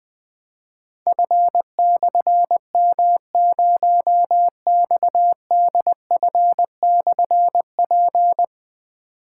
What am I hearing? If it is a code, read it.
Morse code, heard as a keyed tone, about 20 wpm, 700 Hz. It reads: F/M0XDF/P
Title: Difficult Call Signs 20wpm